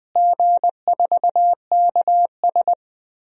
Morse code, heard as a keyed tone, about 21 wpm, 700 Hz. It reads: G4KS
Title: DXCC-100 Call Signs - 4 Characters - Repeated 2x - 20wpm